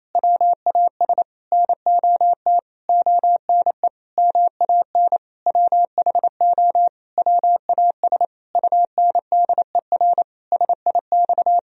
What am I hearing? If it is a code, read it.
Morse code, heard as a keyed tone, about 28 wpm, 700 Hz. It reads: WAS NOT ONE MAN WHO WAS UNDER SIX
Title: Sentences from Top 1000 Words - Review Part 2 - 28wpm